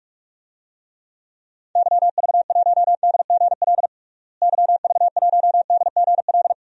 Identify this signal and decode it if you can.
Morse code, heard as a keyed tone, about 41 wpm, 700 Hz. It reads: YV1DGL YV1DGL
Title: DXCC-100 Call Signs - 6 Characters - Repeated 3x - 45wpm